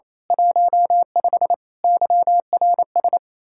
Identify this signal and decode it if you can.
Morse code, heard as a keyed tone, about 27 wpm, 700 Hz. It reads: 15 YRS
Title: Speed-Racing Common QSO Phrases Encore - 28wpm